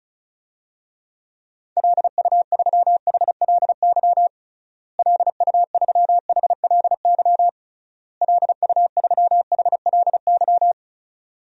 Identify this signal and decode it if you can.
Morse code, heard as a keyed tone, about 34 wpm, 700 Hz. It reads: LU3HLY LU3HLY LU3HLY
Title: DXCC-100 Call Signs - 6 Characters - Repeated 3x - 35wpm